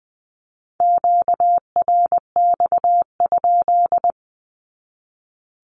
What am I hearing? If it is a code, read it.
Morse code, heard as a keyed tone, about 20 wpm, 700 Hz. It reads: QRX?
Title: Q-Codes 20wpm